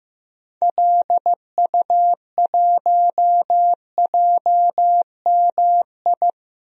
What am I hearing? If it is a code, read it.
Morse code, heard as a keyed tone, about 15 wpm, 700 Hz. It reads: LU1JMI
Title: DXCC-100 Call Signs - 6 Characters - Repeated 2x - 15wpm